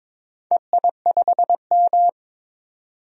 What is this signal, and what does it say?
Morse code, heard as a keyed tone, about 22 wpm, 700 Hz. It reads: EI5M
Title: DXCC-100 Call Signs - 4 Characters - 22wpm